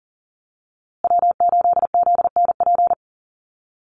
Morse code, heard as a keyed tone, about 40 wpm, 700 Hz. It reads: W8ZNP
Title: USA Call Signs Repeated 3x 40wpm